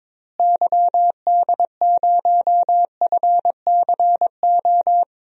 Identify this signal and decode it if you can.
Morse code, heard as a keyed tone, about 22 wpm, 700 Hz. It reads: YD0FCO
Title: DXCC-100 Call Signs - 6 Characters - Repeated 2x - 22wpm